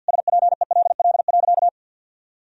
Morse code, heard as a keyed tone, about 48 wpm, 700 Hz. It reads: SPELL.
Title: Sentences from Top 1000 Words - Review Part 1 - 50wpm